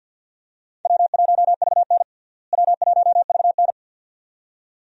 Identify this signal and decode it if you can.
Morse code, heard as a keyed tone, about 51 wpm, 700 Hz. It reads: W1VN W1VN
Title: USA Call Signs - 4 Characters - Repeated 2x - 50wpm